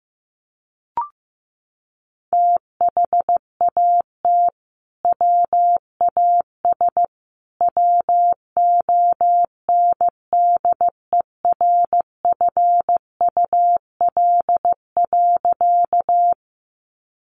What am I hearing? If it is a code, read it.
Morse code, heard as a keyed tone, about 14 wpm, 700 Hz. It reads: THAT WAS WONDERFUL.
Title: Sentences from Top 1100 Words 15wpm